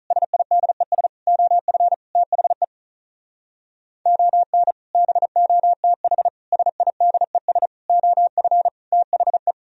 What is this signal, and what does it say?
Morse code, heard as a keyed tone, about 44 wpm, 700 Hz. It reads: SIDES OF THE ON BOTH SIDES OF THE